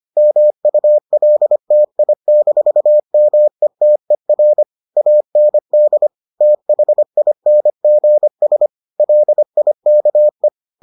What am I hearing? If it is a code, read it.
Morse code, heard as a keyed tone, about 25 wpm, 600 Hz. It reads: MULTI-METER AND THINGS LIKE